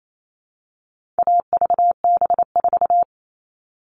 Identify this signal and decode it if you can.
Morse code, heard as a keyed tone, about 28 wpm, 700 Hz. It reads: AVB4
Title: All Cut Numbers - 4-Digit 28wpm